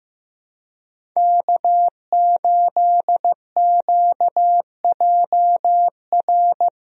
Morse code, heard as a keyed tone, about 16 wpm, 700 Hz. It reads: K8QJR
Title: USA Call Signs - 5 Characters - 15wpm